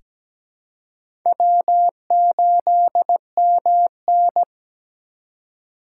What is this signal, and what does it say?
Morse code, heard as a keyed tone, about 17 wpm, 700 Hz. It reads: W8MN
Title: USA Call Signs - 4 Characters - 2x Character Spacing - 17wpm